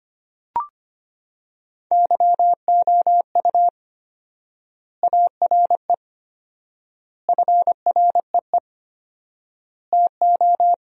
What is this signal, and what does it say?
Morse code, heard as a keyed tone, about 25 wpm, 700 Hz. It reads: YOU ARE FREE TO
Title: 4x Word-Spacing - Sentences from Top 500 Words - 25wpm